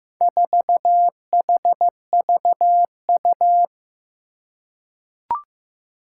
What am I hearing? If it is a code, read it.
Morse code, heard as a keyed tone, about 15 wpm, 700 Hz. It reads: 4HVU
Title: Four Challenging Letter-Number Combinations 15wpm